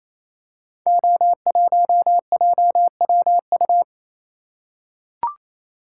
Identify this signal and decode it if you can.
Morse code, heard as a keyed tone, about 27 wpm, 700 Hz. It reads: O1JWU
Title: Code Groups - 5 Letter:Numbers - 28wpm